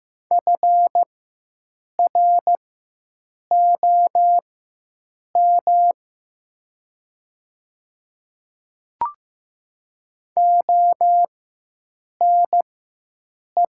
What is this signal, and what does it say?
Morse code, heard as a keyed tone, about 15 wpm, spaced as for 7 wpm, 700 Hz. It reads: FROM ONE
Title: 4x Character Spacing - Top 300 Words - 15wpm